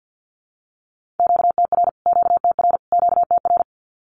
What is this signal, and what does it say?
Morse code, heard as a keyed tone, about 50 wpm, 700 Hz. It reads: QTR QTR QTR
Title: Q-Codes Repeated 3x 50wpm